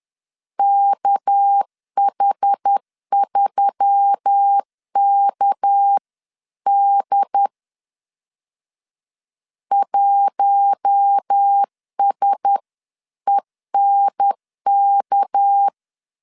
Morse code, heard as a keyed tone, about 10 wpm, 800 Hz. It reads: KH3KD 1SENK
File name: LEZIONI / 6 PKX0 / 6 PK 10wpm 2436